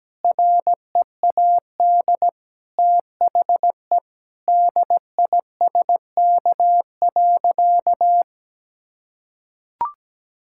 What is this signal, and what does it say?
Morse code, heard as a keyed tone, about 17 wpm, 700 Hz. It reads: READ THE DISK.
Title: Sentences from Top 1000 Words - Review Part 6 - 17wpm